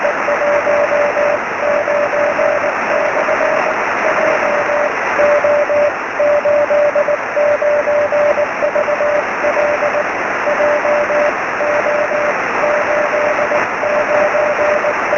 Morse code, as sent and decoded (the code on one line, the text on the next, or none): none